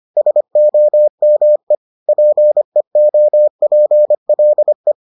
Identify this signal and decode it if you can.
Morse code, heard as a keyed tone, about 25 wpm, 600 Hz. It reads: SOME PEOPLE